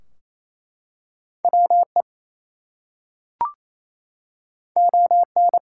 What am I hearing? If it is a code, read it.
Morse code, heard as a keyed tone, about 27 wpm, 700 Hz. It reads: WE ON
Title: Two Letter Words - 28wpm